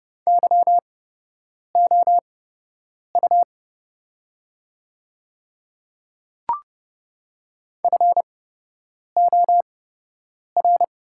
Morse code, heard as a keyed tone, about 30 wpm, spaced as for 8 wpm, 690 Hz. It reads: YOU FOR
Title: 8x Character Spacing - Top 300 Words - 30wpm